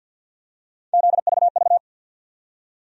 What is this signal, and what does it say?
Morse code, heard as a keyed tone, about 47 wpm, 700 Hz. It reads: ZVV
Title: USA Call Sign Postfixes - 3 Letters 50wpm